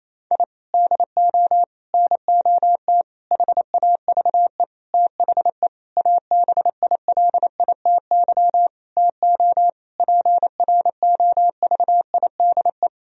Morse code, heard as a keyed tone, about 28 wpm, 700 Hz. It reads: I DO NOT HAVE THE ABILITY TO PROVIDE
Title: Sentences from Top 1000 Words - Review Part 3 - 28wpm